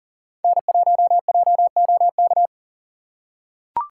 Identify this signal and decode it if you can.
Morse code, heard as a keyed tone, about 40 wpm, 700 Hz. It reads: N1JOK
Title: USA Call Signs - 5 Characters - 3x Character Spacing - 40wpm